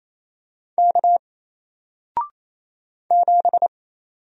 Morse code, heard as a keyed tone, about 27 wpm, 700 Hz. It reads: K 7
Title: Single Character (Letter:Number:Punctuation:Prosign) Alt 28wpm